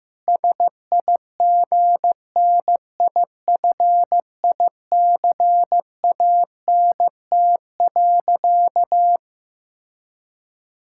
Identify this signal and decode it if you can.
Morse code, heard as a keyed tone, about 15 wpm, 700 Hz. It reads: SIGNIFICANT.
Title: Sentences from Top 1000 Words - Review Part 6 - 15wpm